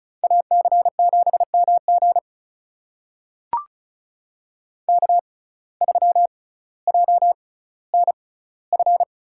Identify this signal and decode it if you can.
Morse code, heard as a keyed tone, about 36 wpm, spaced as for 18 wpm, 700 Hz. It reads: AC7MG K3JNF
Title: USA Call Signs - 5 Characters - 6x Character Spacing - 35wpm